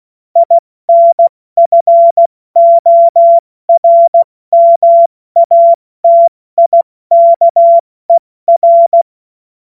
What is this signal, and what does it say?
Morse code, heard as a keyed tone, about 16 wpm, spaced as for 14 wpm, 680 Hz. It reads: INFORMATIKER
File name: Morsekurs / Witze / Witz 12. April 2022 70 F80